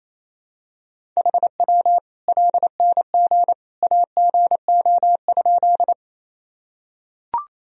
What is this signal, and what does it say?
Morse code, heard as a keyed tone, about 28 wpm, 700 Hz. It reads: HW LNG AGO?
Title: Sentences from Top 500 Words and Abbreviations 28wpm